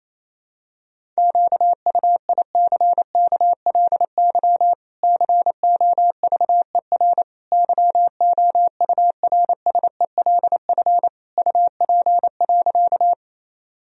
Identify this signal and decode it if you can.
Morse code, heard as a keyed tone, about 28 wpm, 700 Hz. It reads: QUICKLY COVER YOURSELF UP.
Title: Sentences from Top 1000 Words - Review Part 1 - 28wpm